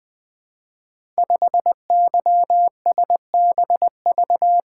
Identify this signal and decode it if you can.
Morse code, heard as a keyed tone, about 20 wpm, 700 Hz. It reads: 5YSBV
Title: Code Groups - 5 Letter:Numbers - 20wpm